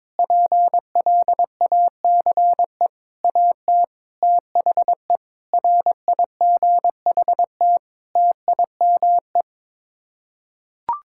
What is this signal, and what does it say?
Morse code, heard as a keyed tone, about 22 wpm, 700 Hz. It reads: PLACE AT THE RIGHT TIME
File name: Sets of 9 Words - 22wpm